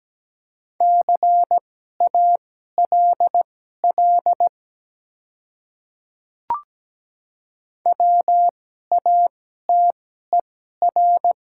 Morse code, heard as a keyed tone, about 17 wpm, spaced as for 12 wpm, 700 Hz. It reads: CALL WATER
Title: 2x Character Spacing - Top 200 Words - 17wpm